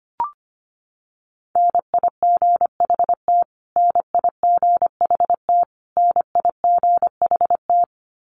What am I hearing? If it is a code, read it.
Morse code, heard as a keyed tone, about 25 wpm, 700 Hz. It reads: NIGHT NIGHT NIGHT
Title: Top 500 Words Repeated 25wpm